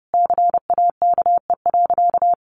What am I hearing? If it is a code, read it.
Morse code, heard as a keyed tone, about 30 wpm, 700 Hz. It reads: CAKE.
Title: Easy Phrases 30wpm